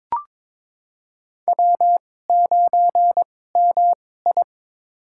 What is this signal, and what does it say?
Morse code, heard as a keyed tone, about 21 wpm, spaced as for 16 wpm, 700 Hz. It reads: W9MI
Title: USA Call Signs - 4 Characters - 2x Character Spacing - 22wpm